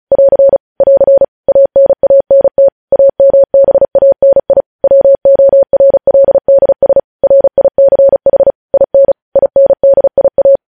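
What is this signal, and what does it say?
Morse code, heard as a keyed tone, about 36 wpm, 550 Hz. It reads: <AR> <AR> ANANT AMBANI WORLDS RICH IN INDIA
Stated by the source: Morse code practice sending news headlines at 35 wpm.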